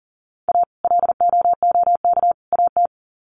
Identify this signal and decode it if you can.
Morse code, heard as a keyed tone, about 40 wpm, 700 Hz. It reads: A LOOK AT
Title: Sets of 3 Words 40wpm